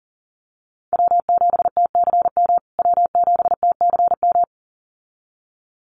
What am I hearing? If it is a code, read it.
Morse code, heard as a keyed tone, about 40 wpm, 700 Hz. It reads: W7TCM W7TCM
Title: USA Call Signs Repeated 2x 40wpm